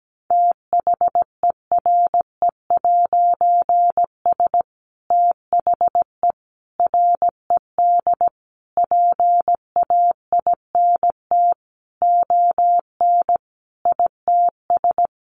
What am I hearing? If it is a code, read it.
Morse code, heard as a keyed tone, about 17 wpm, 700 Hz. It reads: THERE'S THE RED PAINT ON ITS
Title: A Sound Of Thunder 17wpm